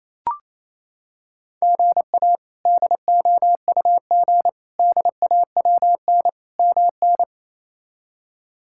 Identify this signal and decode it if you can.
Morse code, heard as a keyed tone, about 28 wpm, 700 Hz. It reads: GA DOUG DAWN MN
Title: K1USN SST QSO Practice - 28wpm